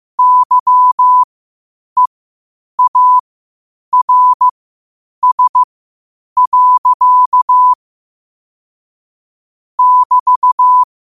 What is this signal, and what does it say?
Morse code, heard as a keyed tone, about 15 wpm, spaced as for 8 wpm, 1000 Hz. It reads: YEARS. <BT>
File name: 8wpm